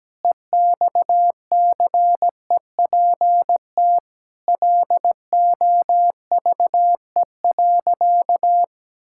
Morse code, heard as a keyed tone, about 17 wpm, 700 Hz. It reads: EXCEPT LOVE.
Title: Sentences from Top 1600 Words 17wpm